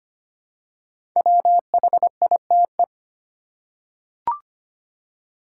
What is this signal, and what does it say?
Morse code, heard as a keyed tone, about 25 wpm, 700 Hz. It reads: WHITE
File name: 5 Letter Words - First 3 Letters Given - 25wpm